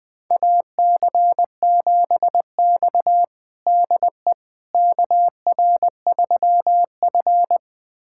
Morse code, heard as a keyed tone, about 21 wpm, 700 Hz. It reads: AC7X DE KR3F
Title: USA Callsign DE Callsign - 4 Characters - 20wpm